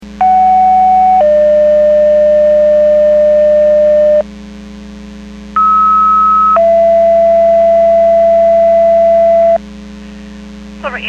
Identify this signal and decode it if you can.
Two-tone page, heard as a one-second tone, then a three-second tone, about 750 then 600 Hz, about 1290 then 670 Hz, with speech. It is heard from 0.2 to 9.6 s.